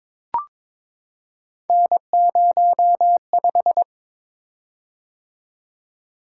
Morse code, heard as a keyed tone, about 22 wpm, 700 Hz. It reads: N05
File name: practice / 22wpm / Three Letter-Number Combinations - 22wpm